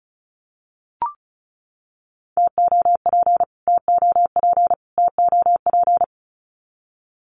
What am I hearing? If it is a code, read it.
Morse code, heard as a keyed tone, about 34 wpm, 700 Hz. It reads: TOP TOP TOP
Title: Top 500 Words Repeated 35wpm